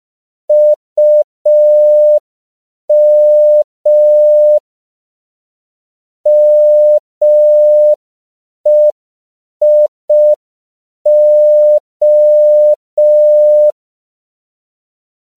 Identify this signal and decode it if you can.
Morse code, heard as a keyed tone, about 5 wpm, 600 Hz. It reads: UM MEIO
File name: CW / 5PPM